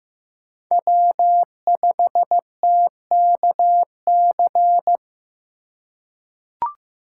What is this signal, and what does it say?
Morse code, heard as a keyed tone, about 15 wpm, 700 Hz. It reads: W5TKC
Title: USA Call Signs - 5 Characters - 2x Character Spacing - 15wpm